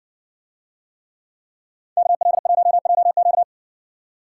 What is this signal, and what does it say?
Morse code, heard as a keyed tone, about 58 wpm, 700 Hz. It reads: KL1JQ